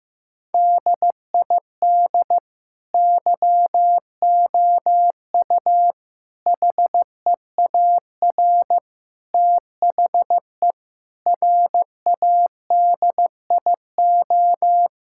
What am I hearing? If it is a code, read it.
Morse code, heard as a keyed tone, about 15 wpm, 700 Hz. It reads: DID YOU HEAR THE RADIO
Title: Sentences from Top 1000 Words - Review Part 5 - 15wpm